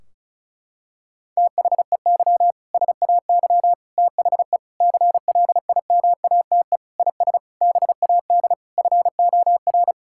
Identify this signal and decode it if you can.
Morse code, heard as a keyed tone, about 36 wpm, 700 Hz. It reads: THEY SAY THE CLIMATE IS BAD FOR
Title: Sentences from Top 1600 Words 35wpm